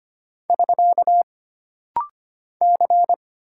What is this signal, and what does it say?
Morse code, heard as a keyed tone, about 25 wpm, 700 Hz. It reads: <SK> C
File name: Single Character (Letter:Number:Punctuation:Prosign) - Rapid-Fire - 25wpm